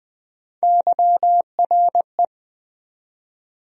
Morse code, heard as a keyed tone, about 21 wpm, 700 Hz. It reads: YRE
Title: USA Call Sign Postfixes - 3 Letters 20wpm